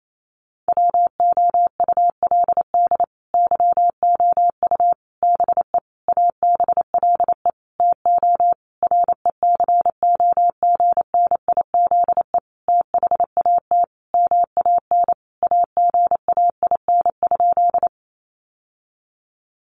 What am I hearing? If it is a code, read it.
Morse code, heard as a keyed tone, about 28 wpm, 700 Hz. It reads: WOULD YOU BE ABLE TO RECOGNIZE THAT MAN AGAIN?
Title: Sentences from Top 1400 Words 28wpm